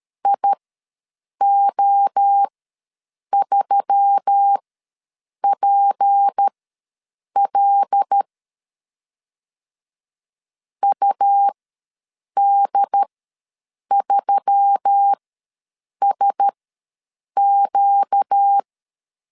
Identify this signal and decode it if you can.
Morse code, heard as a keyed tone, about 13 wpm, spaced as for 7 wpm, 800 Hz. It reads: IO3PL UD3SQ
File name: LEZIONI / 6 PKX0 / 6 P ws 12wpm 3117